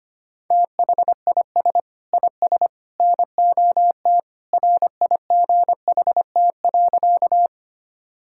Morse code, heard as a keyed tone, about 25 wpm, 700 Hz. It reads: THIS IS NOT RIGHT.
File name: practice / 25wpm / Sentences with 3-Word Phrases - 25wpm